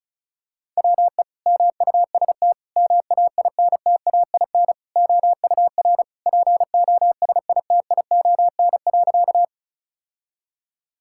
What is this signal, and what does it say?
Morse code, heard as a keyed tone, about 34 wpm, 700 Hz. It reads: WE MUST MAINTAIN OUR POSITION.
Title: Sentences from Top 1000 Words - Review Part 5 - 35wpm